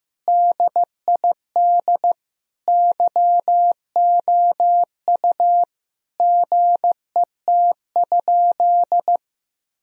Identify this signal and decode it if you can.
Morse code, heard as a keyed tone, about 15 wpm, 700 Hz. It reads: DID YOU GET?
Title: Sentences from Top 1000 Words - Review Part 3 - 15wpm